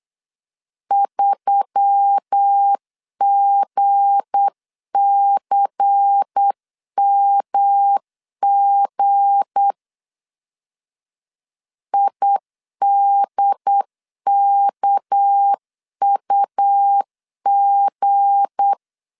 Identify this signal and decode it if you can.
Morse code, heard as a keyed tone, about 8 wpm, 800 Hz. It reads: 3GCMG IDKUG
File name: LEZIONI / 7 GF9Z / 7 G 8wpm 3118